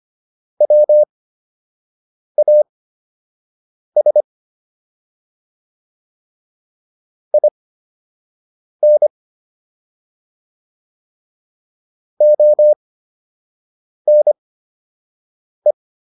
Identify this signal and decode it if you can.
Morse code, heard as a keyed tone, about 25 wpm, spaced as for 6 wpm, 600 Hz. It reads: WAS IN ONE